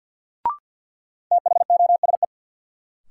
Morse code, heard as a keyed tone, about 50 wpm, 700 Hz. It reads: THOSE
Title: Sentences from Top 300 Words - New Word Drill - 50wpm